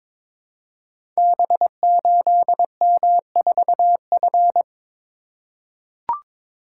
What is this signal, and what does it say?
Morse code, heard as a keyed tone, about 22 wpm, 700 Hz. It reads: B8M4F
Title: Code Groups - 5 Letter:Numbers - 22wpm